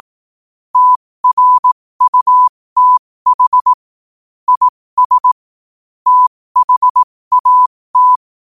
Morse code, heard as a keyed tone, about 18 wpm, spaced as for 15 wpm, 1000 Hz. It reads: TRUTH IS THAT
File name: 15wpm